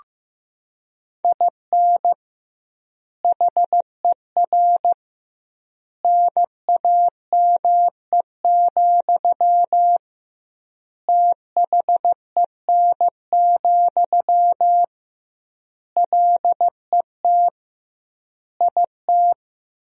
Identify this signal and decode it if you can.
Morse code, heard as a keyed tone, about 15 wpm, 700 Hz. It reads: IN HER NAME, THEN, LET IT
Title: 2x Word-Spacing - Sentences from Top 500 Words - 15wpm